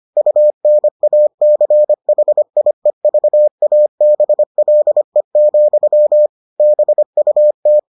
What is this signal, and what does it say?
Morse code, heard as a keyed tone, about 25 wpm, 600 Hz. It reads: UNACHIEVABLE, BUT